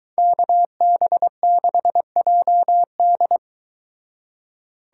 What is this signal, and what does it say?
Morse code, heard as a keyed tone, about 22 wpm, 700 Hz. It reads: KB6JD